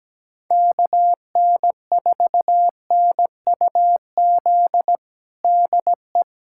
Morse code, heard as a keyed tone, about 17 wpm, 700 Hz. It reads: KN4NUZ DE
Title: USA Callsign DE Callsign 17wpm